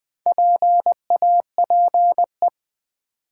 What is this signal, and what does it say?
Morse code, heard as a keyed tone, about 20 wpm, 700 Hz. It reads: PAPE
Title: Build A Word - 1 Letter At A Time - 5-Letter Words - 20wpm